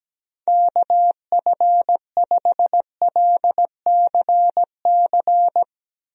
Morse code, heard as a keyed tone, about 17 wpm, 700 Hz. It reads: KF5LCC
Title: USA Call Signs - 6 Characters - Repeated 2x - 17wpm